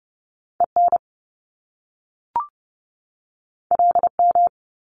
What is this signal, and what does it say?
Morse code, heard as a keyed tone, about 30 wpm, 700 Hz. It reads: EN LM
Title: Top Two Letter Combinations in Words 30wpm